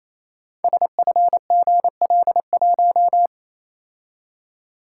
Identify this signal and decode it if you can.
Morse code, heard as a keyed tone, about 27 wpm, 700 Hz. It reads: SFGL1
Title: Code Groups - 5 Letter:Numbers - 28wpm